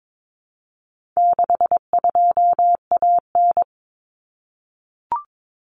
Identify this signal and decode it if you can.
Morse code, heard as a keyed tone, about 22 wpm, 700 Hz. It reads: 62AN
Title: Common Cut Numbers - 4-Digit 22wpm